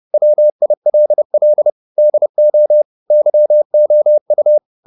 Morse code, heard as a keyed tone, about 30 wpm, 600 Hz. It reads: WILL DO YOU
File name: the old man and the sea / the old man and the sea 1 30wpm